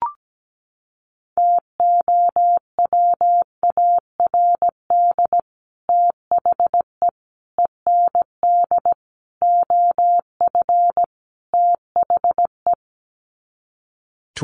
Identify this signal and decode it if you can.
Morse code, heard as a keyed tone, about 17 wpm, 700 Hz. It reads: TOWARD THE END OF THE
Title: Sets of 5 Words 17wpm